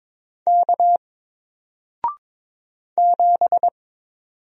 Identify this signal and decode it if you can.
Morse code, heard as a keyed tone, about 22 wpm, 700 Hz. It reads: K 7
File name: Single-Letter-Number-22wpm